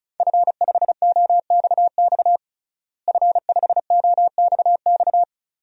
Morse code, heard as a keyed tone, about 36 wpm, 700 Hz. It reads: F5OXX F5OXX
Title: DXCC-100 Call Signs - 5 Characters - Repeated 2x - 35wpm